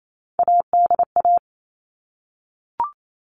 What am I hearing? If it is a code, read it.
Morse code, heard as a keyed tone, about 28 wpm, 700 Hz. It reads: ADA
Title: All Cut Numbers - 3-Digit 28wpm